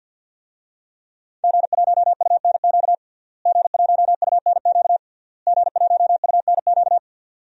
Morse code, heard as a keyed tone, about 53 wpm, 700 Hz. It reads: G1UNQ G1UNQ G1UNQ
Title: DXCC-100 Call Signs - 5 Characters - Repeated 3x - 50wpm